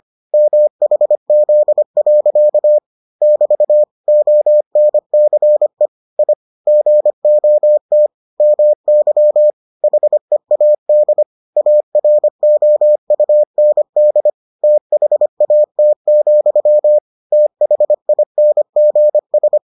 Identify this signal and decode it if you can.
Morse code, heard as a keyed tone, about 25 wpm, 600 Hz. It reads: MHZ. <BT> ONCE I GOT MY HEAD AROUND THAT, THINGS